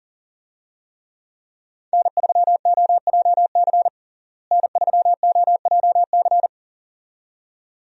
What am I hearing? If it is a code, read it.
Morse code, heard as a keyed tone, about 40 wpm, 700 Hz. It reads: N3OJC N3OJC
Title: USA Call Signs - 5 Characters - Repeated 2x - 40wpm